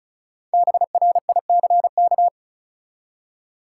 Morse code, heard as a keyed tone, about 36 wpm, 700 Hz. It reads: BRICK
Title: Five Letter Words - 35wpm